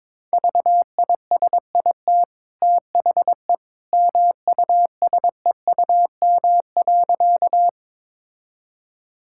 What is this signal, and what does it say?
Morse code, heard as a keyed tone, about 22 wpm, 690 Hz. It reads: VISIT THE MUSEUM.
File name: practice / 22wpm / Easy Sentences with Hints - 22wpm